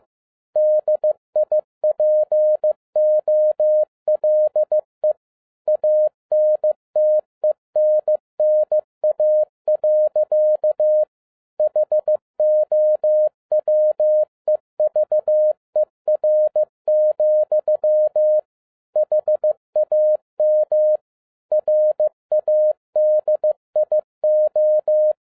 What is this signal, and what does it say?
Morse code, heard as a keyed tone, about 15 wpm, 610 Hz. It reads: DIPOLE ANTENNA. HOWEVER, HAM RADIO